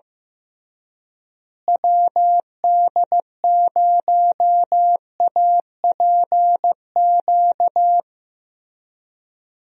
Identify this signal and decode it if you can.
Morse code, heard as a keyed tone, about 16 wpm, 700 Hz. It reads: WD0APQ
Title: USA Call Signs - 6 Characters - Repeated 3x - 15wpm